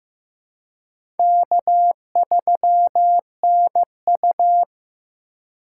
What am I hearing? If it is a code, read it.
Morse code, heard as a keyed tone, about 15 wpm, 700 Hz. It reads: K3NU
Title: USA Call Signs - 4 Characters - 3x Character Spacing - 15wpm